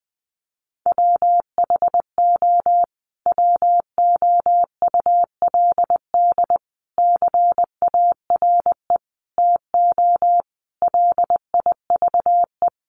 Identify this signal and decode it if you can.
Morse code, heard as a keyed tone, about 20 wpm, 700 Hz. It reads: WHO WOULD CARE TO LIVE
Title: Sentences from Top 1500 Words 20wpm